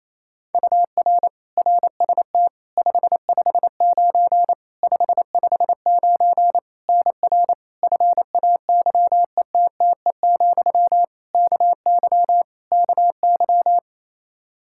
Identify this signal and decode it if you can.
Morse code, heard as a keyed tone, about 28 wpm, 700 Hz. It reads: UR RST 559 559 NR FAYETTE, KY KY
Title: Practice QSO - Exchange 1 - Beginning - 28wpm